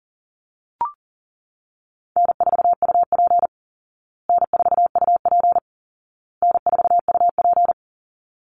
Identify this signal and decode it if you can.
Morse code, heard as a keyed tone, about 40 wpm, 700 Hz. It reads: N4UP N4UP N4UP
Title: Call Signs Repeated 2x - 40wpm